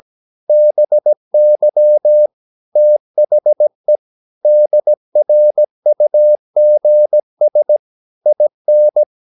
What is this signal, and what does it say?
Morse code, heard as a keyed tone, about 17 wpm, 600 Hz. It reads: BY THE DRUGS IN
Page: Romanen Never af Ken Follet på engelsk.